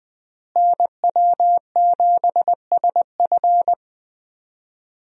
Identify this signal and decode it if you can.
Morse code, heard as a keyed tone, about 21 wpm, 700 Hz. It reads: NW7SF
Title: USA Call Signs - 5 Characters - Repeated 3x - 20wpm